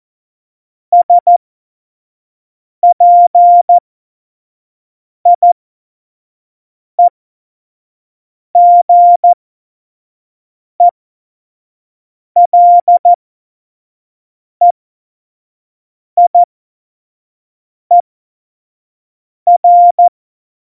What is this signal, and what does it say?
Morse code, heard as a keyed tone, about 14 wpm, spaced as for 5 wpm, 700 Hz. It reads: SPIEGELEIER
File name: Morsekurs / Witze / Witz 17. September 2019, 25 F70